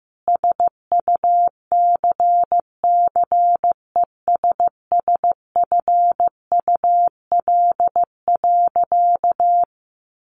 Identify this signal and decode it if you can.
Morse code, heard as a keyed tone, about 15 wpm, 700 Hz. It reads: SUCCESSFUL.
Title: Sentences from Top 1400 Words 15wpm